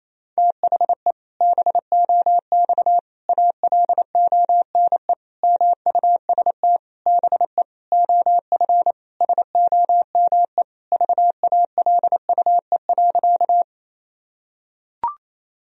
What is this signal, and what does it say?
Morse code, heard as a keyed tone, about 28 wpm, 700 Hz. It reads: THE BOX ALONE MUST BE OF SOME VALUE.
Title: Sentences from Top 1000 Words - Review Part 3 - 28wpm